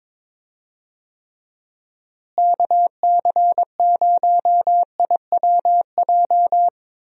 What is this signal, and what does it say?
Morse code, heard as a keyed tone, about 21 wpm, 700 Hz. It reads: KC0IWJ
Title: USA Call Signs - 6 Characters - Repeated 3x - 22wpm